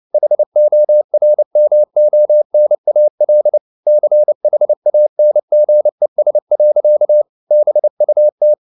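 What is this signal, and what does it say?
Morse code, heard as a keyed tone, about 29 wpm, 600 Hz. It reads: HORMONAL CHANGES. BUT
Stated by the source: Morse code at 29 wpm, sending Ken Follett's novel Never in English, from CW med Gnister.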